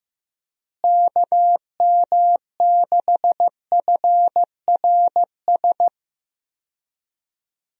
Morse code, heard as a keyed tone, about 15 wpm, 700 Hz. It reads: KM6FRS
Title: USA Call Signs - 6 Characters - Repeated 3x - 15wpm